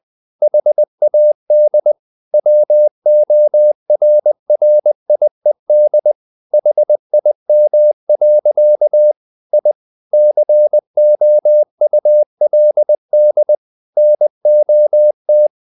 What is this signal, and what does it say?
Morse code, heard as a keyed tone, about 20 wpm, 600 Hz. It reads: HAD WORRIED HIM. I COULD NOT
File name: the old man and the sea part 2 20wpm